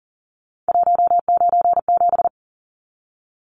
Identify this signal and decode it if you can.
Morse code, heard as a keyed tone, about 40 wpm, 700 Hz. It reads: 197
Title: Numbers - 3-Digit 40wpm